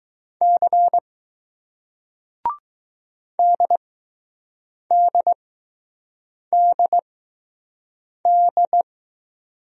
Morse code, heard as a keyed tone, about 20 wpm, spaced as for 6 wpm, 700 Hz. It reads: C DDDD